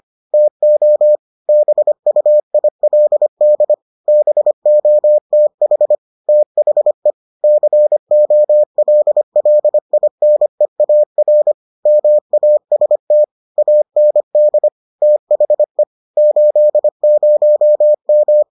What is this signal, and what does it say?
Morse code, heard as a keyed tone, about 25 wpm, 600 Hz. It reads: TO BUILD BOTH THE COLLINEAR MAST AND THE 80M